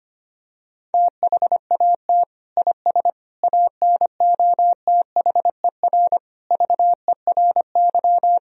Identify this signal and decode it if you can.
Morse code, heard as a keyed tone, about 25 wpm, 700 Hz. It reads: THAT IS ANOTHER VERY
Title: Sentences from Top 1000 Words - Review Part 2 - 25wpm